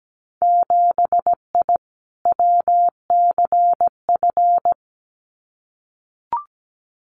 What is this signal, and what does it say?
Morse code, heard as a keyed tone, about 17 wpm, 700 Hz. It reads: 7I WCF
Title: Winter Field Day Exchanges Encore 17wpm